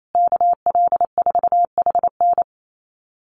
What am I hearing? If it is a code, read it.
Morse code, heard as a keyed tone, about 28 wpm, 700 Hz. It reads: KL4HN
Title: USA Call Signs Repeated 3x 28wpm